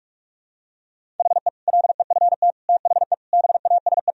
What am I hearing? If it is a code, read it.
Morse code, heard as a keyed tone, about 46 wpm, 700 Hz. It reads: HE LEFT THE BASE
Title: Sentences from Top 1000 Words - Review Part 1 - 45wpm